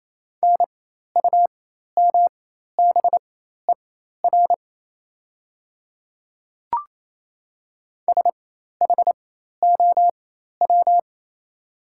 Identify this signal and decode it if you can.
Morse code, heard as a keyed tone, about 27 wpm, spaced as for 13 wpm, 700 Hz. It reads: NUMBER SHOW
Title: 4x Character Spacing - Top 200 Words - 28wpm